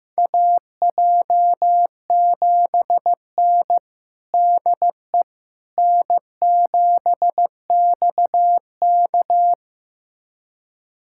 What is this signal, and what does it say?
Morse code, heard as a keyed tone, about 15 wpm, 700 Hz. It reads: AJ7N DE N7XK
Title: USA Callsign DE Callsign - 4 Characters - 15wpm